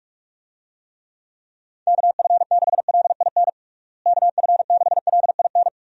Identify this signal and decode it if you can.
Morse code, heard as a keyed tone, about 46 wpm, 700 Hz. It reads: KF6LIN KF6LIN
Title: USA Call Signs - 6 Characters - Repeated 2x - 45wpm